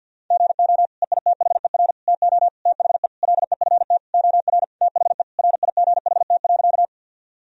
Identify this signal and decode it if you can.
Morse code, heard as a keyed tone, about 50 wpm, 700 Hz. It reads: GO EITHER TO THE LEFT OR THE RIGHT.
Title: Sentences from Top 1000 Words - Review Part 4 - 50wpm